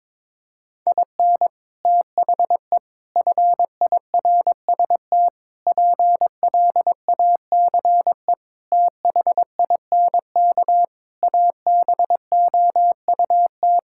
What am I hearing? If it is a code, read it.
Morse code, heard as a keyed tone, about 22 wpm, 700 Hz. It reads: IN THE FIRST PLACE THINK ABOUT
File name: practice / 22wpm / Sentences with 3-Word Phrases - 22wpm